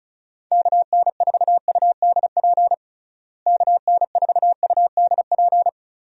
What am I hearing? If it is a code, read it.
Morse code, heard as a keyed tone, about 36 wpm, 700 Hz. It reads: KN4UDP KN4UDP
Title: USA Call Signs - 6 Characters - Repeated 3x - 35wpm